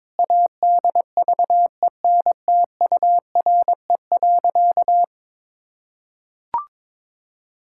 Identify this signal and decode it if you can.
Morse code, heard as a keyed tone, about 22 wpm, 700 Hz. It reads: ADVENTURE.
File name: practice / 22wpm / Sentences with 4-Word Phrases - 22wpm